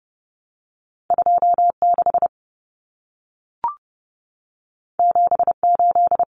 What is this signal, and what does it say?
Morse code, heard as a keyed tone, about 30 wpm, 700 Hz. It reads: 26 78
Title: Numbers - 2-Digit 30wpm